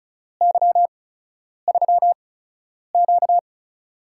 Morse code, heard as a keyed tone, about 36 wpm, spaced as for 10 wpm, 700 Hz. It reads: Y3Q
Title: USA Call Signs - 4 Characters - 8x Character Spacing - 35wpm